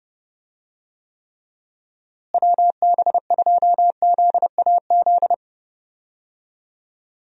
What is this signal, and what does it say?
Morse code, heard as a keyed tone, about 30 wpm, 700 Hz. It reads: WB2ZAZ
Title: USA Call Signs - 6 Characters - Repeated 3x - 30wpm